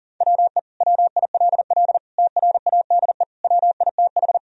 Morse code, heard as a keyed tone, about 40 wpm, 700 Hz. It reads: WE WILL TRADE WITH
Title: Sentences from Top 1000 Words - Review Part 4 - 40wpm